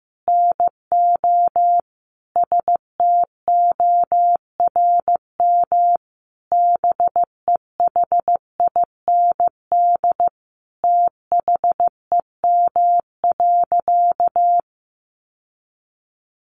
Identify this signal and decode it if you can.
Morse code, heard as a keyed tone, about 15 wpm, 700 Hz. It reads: NO STORM BEHIND THEM.
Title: Sentences from Top 1500 Words 15wpm